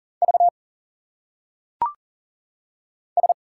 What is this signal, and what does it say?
Morse code, heard as a keyed tone, about 40 wpm, 700 Hz. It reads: V S
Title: Single Letters - Phonetic Pronunciation 40wpm